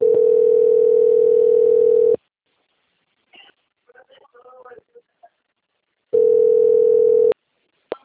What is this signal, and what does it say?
Ringback tone (North America), 440 + 480 Hz, with speech.
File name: out- - - - - .50388